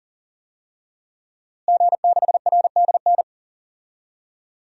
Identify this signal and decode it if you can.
Morse code, heard as a keyed tone, about 40 wpm, 700 Hz. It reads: G6RDN
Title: DXCC-100 Call Signs - 5 Characters - Repeated 2x - 40wpm